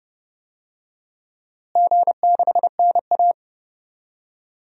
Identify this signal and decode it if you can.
Morse code, heard as a keyed tone, about 30 wpm, 700 Hz. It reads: G6NA
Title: DXCC-100 Call Signs - 4 Characters - Repeated 2x - 30wpm